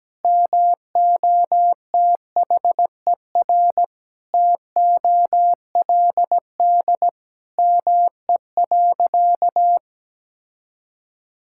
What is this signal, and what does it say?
Morse code, heard as a keyed tone, about 17 wpm, 700 Hz. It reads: MOTHER TOLD ME.
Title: Sentences from Top 1000 Words - Review Part 6 - 17wpm